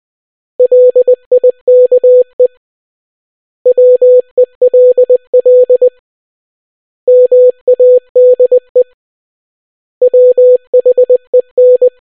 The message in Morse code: LIKE WELL MADE WHEN